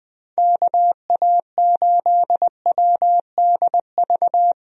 Morse code, heard as a keyed tone, about 20 wpm, 700 Hz. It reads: KA8WDV
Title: USA Call Signs - 6 Characters - Repeated 3x - 20wpm